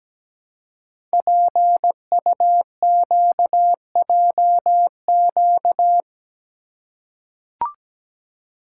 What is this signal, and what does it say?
Morse code, heard as a keyed tone, about 17 wpm, 700 Hz. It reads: PUQJQ
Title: Code Groups - 5 Letters - 17wpm